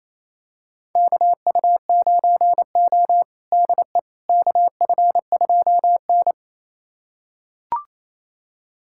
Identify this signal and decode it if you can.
Morse code, heard as a keyed tone, about 28 wpm, 700 Hz. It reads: KU9O DE KF2N
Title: USA Callsign DE Callsign - 4 Characters - 28wpm